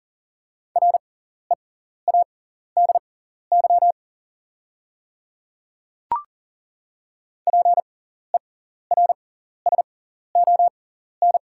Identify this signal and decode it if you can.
Morse code, heard as a keyed tone, about 40 wpm, spaced as for 14 wpm, 690 Hz. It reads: READY PERSON
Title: 6x Character Spacing - Top 500 Words - 40wpm